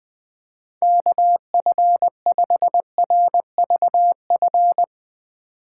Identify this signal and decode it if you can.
Morse code, heard as a keyed tone, about 20 wpm, 700 Hz. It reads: KF5RVF
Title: USA Call Signs - 6 Characters - 8x Character Spacing - 20wpm